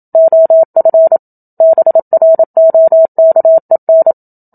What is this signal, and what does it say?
Morse code, heard as a keyed tone, about 27 wpm, 650 Hz. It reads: OF BROKEN
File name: Book Two / 01-Chapter 1 - 27 WPM